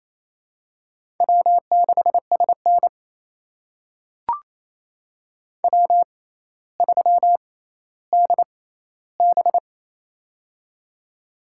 Morse code, heard as a keyed tone, about 28 wpm, spaced as for 14 wpm, 700 Hz. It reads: W6SN W3DB
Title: USA Call Signs - 4 Characters - 6x Character Spacing - 28wpm